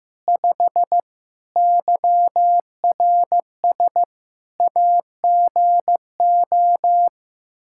Morse code, heard as a keyed tone, about 16 wpm, 700 Hz. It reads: 5 YRS AGO
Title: Practice QSO - Exchange 3 - More Info - 15wpm